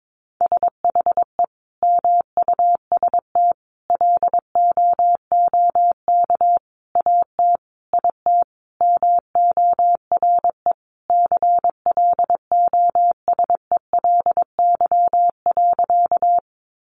Morse code, heard as a keyed tone, about 22 wpm, 700 Hz. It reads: SHE MUST LOOK AT IT MORE CLOSELY.
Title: Sentences from Top 1400 Words 22wpm